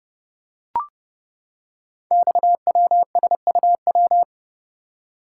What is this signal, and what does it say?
Morse code, heard as a keyed tone, about 30 wpm, 700 Hz. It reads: XWSUW
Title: Code Groups - 5 Letters - 30wpm